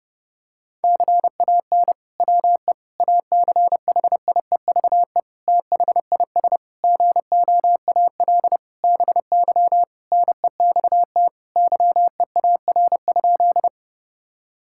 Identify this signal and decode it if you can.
Morse code, heard as a keyed tone, about 30 wpm, 700 Hz. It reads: CAN WE ACHIEVE THIS GOAL BY NEXT YEAR?
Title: Sentences from Top 1000 Words - Review Part 5 - 30wpm